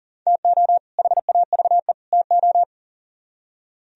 Morse code, heard as a keyed tone, about 51 wpm, 700 Hz. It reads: TO HAVE TO